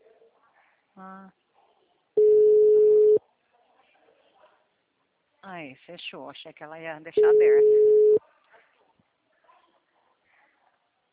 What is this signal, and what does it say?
Ringback tone (Europe, 425 Hz), with speech.